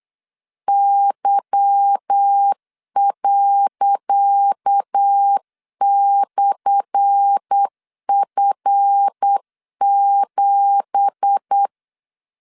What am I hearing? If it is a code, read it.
Morse code, heard as a keyed tone, about 8 wpm, 800 Hz. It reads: Y./F7
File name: LEZIONI / 10 SIGNS / 10C DOT 8wpm 3649